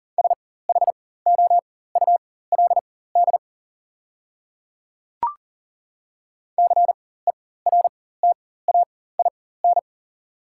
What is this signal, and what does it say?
Morse code, heard as a keyed tone, about 39 wpm, spaced as for 19 wpm, 700 Hz. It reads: SHOULD CERTAIN
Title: 4x Character Spacing - Top 500 Words - 40wpm